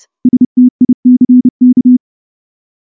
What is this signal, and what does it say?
Morse code, heard as a keyed tone, about 28 wpm, 250 Hz. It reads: STICK
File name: morse / morse-training-30WPM